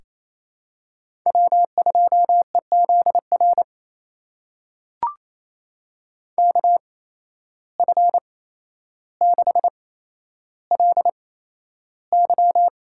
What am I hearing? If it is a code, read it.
Morse code, heard as a keyed tone, about 28 wpm, spaced as for 12 wpm, 700 Hz. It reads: W2EZR KF6LY
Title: USA Call Signs - 5 Characters - 8x Character Spacing - 28wpm